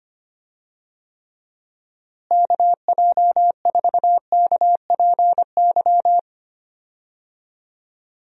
Morse code, heard as a keyed tone, about 25 wpm, 700 Hz. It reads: KJ4KPY
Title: USA Call Signs - 6 Characters - Repeated 2x - 25wpm